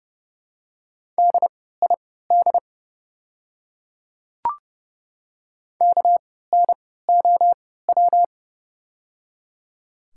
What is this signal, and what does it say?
Morse code, heard as a keyed tone, about 30 wpm, spaced as for 17 wpm, 700 Hz. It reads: DID KNOW
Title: 3x Character Spacing - Top 100 Words - 30wpm